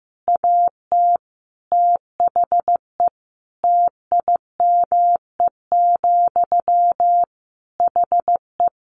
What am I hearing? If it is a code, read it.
Morse code, heard as a keyed tone, about 15 wpm, 700 Hz. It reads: AT THE TIME, HE
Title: Sentences from Top 1200 Words 15wpm